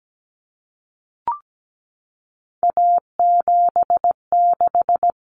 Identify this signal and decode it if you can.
Morse code, heard as a keyed tone, about 17 wpm, 700 Hz. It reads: A76
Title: Common Cut Numbers - 3-Digit 17wpm